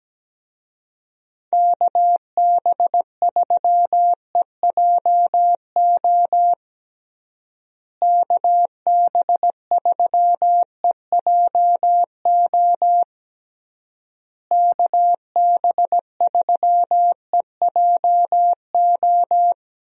Morse code, heard as a keyed tone, about 17 wpm, 700 Hz. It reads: KB3EJO KB3EJO KB3EJO
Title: USA Call Signs - 6 Characters - Repeated 3x - 17wpm